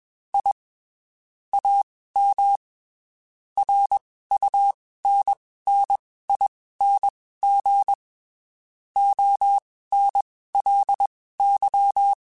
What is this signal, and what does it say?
Morse code, heard as a keyed tone, about 21 wpm, spaced as for 15 wpm, 800 Hz. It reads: I AM RUNNING ONLY
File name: (spaced) / QSOs Spaced-mp3s / WA7QOX 2 @20 sp